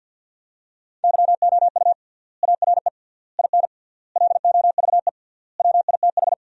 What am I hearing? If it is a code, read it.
Morse code, heard as a keyed tone, about 50 wpm, 700 Hz. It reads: YOU ARE IN LOVE WITH
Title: 3x Word-Spacing - Sentences from Top 400 Words - 50wpm